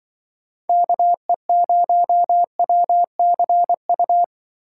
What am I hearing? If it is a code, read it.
Morse code, heard as a keyed tone, about 24 wpm, 700 Hz. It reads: KE0WCU
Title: Speed Racing - USA 2x3 Call Signs 20wpm